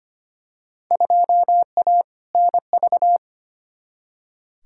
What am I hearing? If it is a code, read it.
Morse code, heard as a keyed tone, about 25 wpm, 700 Hz. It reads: 2A NV
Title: Field Day Exchanges - 25wpm